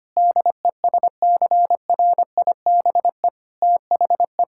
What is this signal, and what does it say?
Morse code, heard as a keyed tone, about 25 wpm, 700 Hz. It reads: DESCRIBE THE
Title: Sentences from Top 1600 Words 25wpm